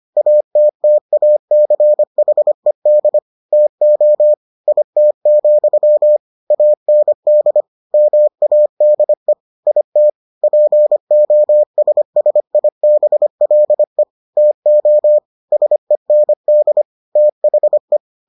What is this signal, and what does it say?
Morse code, heard as a keyed tone, about 25 wpm, 600 Hz. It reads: ATTACHED TO IT, AND MADE IT POSSIBLE TO SEND THE